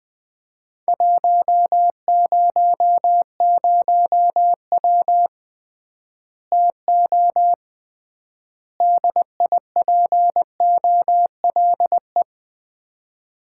Morse code, heard as a keyed tone, about 20 wpm, 700 Hz. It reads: 100W TO DIPOLE
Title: Common QSO Phrases - 3x Word-Spacing - 20wpm